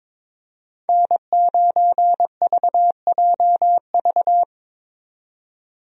Morse code, heard as a keyed tone, about 22 wpm, 700 Hz. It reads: N9VJV
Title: USA Call Signs - 5 Characters - 22wpm